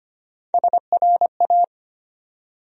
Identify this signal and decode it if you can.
Morse code, heard as a keyed tone, about 25 wpm, 700 Hz. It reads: SRA
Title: USA Call Sign Postfixes - 3 Letters 25wpm